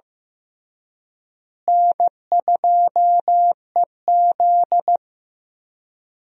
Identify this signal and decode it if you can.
Morse code, heard as a keyed tone, about 15 wpm, 700 Hz. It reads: N2EZ
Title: USA Call Signs - 4 Characters - Repeated 3x - 15wpm